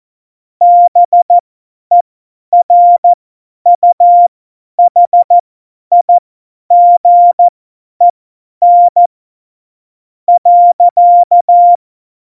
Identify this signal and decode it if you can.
Morse code, heard as a keyed tone, about 14 wpm, spaced as for 10 wpm, 700 Hz. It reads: BERUHIGEN .
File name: Morsekurs / Witze / Witz 13 March 2018, 50 F70